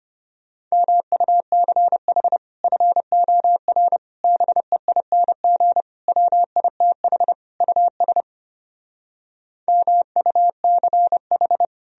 Morse code, heard as a keyed tone, about 33 wpm, 700 Hz. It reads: MUCH FOR BEING WITH US MUCH